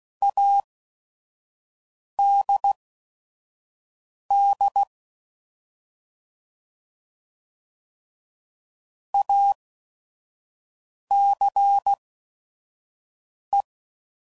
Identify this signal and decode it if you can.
Morse code, heard as a keyed tone, about 16 wpm, spaced as for 5 wpm, 800 Hz. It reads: ADD ACE
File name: Prefixes / By First Letter / As